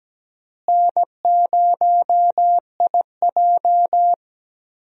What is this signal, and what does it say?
Morse code, heard as a keyed tone, about 17 wpm, 700 Hz. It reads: N0IJ
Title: USA Call Signs - 4 Characters - Repeated 3x - 17wpm